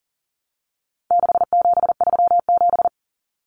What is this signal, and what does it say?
Morse code, heard as a keyed tone, about 40 wpm, 700 Hz. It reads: 6737
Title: Common Cut Numbers - 4-Digit 40wpm